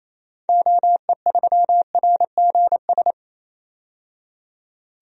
Morse code, heard as a keyed tone, about 28 wpm, 700 Hz. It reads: OE3RGS
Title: DXCC-100 Call Signs - 6 Characters - 28wpm